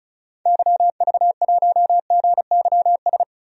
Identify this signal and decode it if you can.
Morse code, heard as a keyed tone, about 34 wpm, 700 Hz. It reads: YV1GYS
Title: DXCC-100 Call Signs - 6 Characters - 35wpm